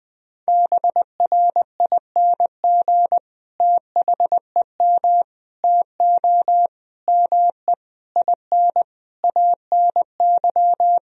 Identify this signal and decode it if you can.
Morse code, heard as a keyed tone, about 20 wpm, 700 Hz. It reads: BRING THEM TO ME IN ANY
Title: Sentences from Top 1000 Words - Review Part 3 - 20wpm